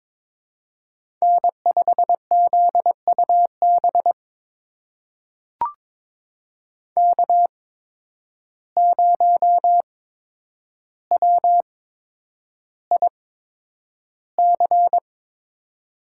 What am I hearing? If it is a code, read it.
Morse code, heard as a keyed tone, about 22 wpm, spaced as for 9 wpm, 700 Hz. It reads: N5ZUB K0WIC
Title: USA Call Signs - 5 Characters - 8x Character Spacing - 22wpm